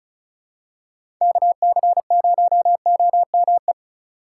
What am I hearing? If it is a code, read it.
Morse code, heard as a keyed tone, about 36 wpm, 700 Hz. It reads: KC0OME
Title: USA Call Signs - 6 Characters - 3x Character Spacing - 35wpm